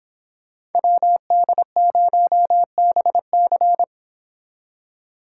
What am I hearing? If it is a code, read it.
Morse code, heard as a keyed tone, about 29 wpm, 700 Hz. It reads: WD0BC